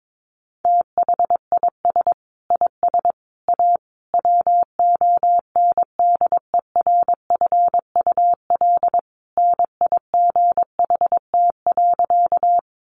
Morse code, heard as a keyed tone, about 22 wpm, 700 Hz. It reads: THIS IS A WONDERFUL NIGHT.
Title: Sentences from Top 1100 Words 22wpm